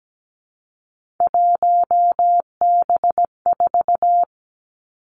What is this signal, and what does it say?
Morse code, heard as a keyed tone, about 17 wpm, 700 Hz. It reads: 1B4
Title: Three Challenging Letter-Number Combinations 17wpm